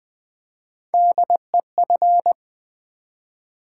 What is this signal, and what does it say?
Morse code, heard as a keyed tone, about 20 wpm, 700 Hz. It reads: DEF
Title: Build A Word - 3 Letters At A Time - 9-Letter Words - 20wpm